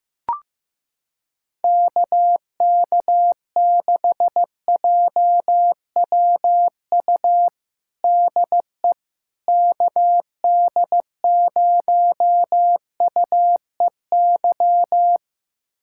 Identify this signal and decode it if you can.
Morse code, heard as a keyed tone, about 15 wpm, 700 Hz. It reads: KK6JWU DE KD0UEY
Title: USA Callsign DE Callsign 15wpm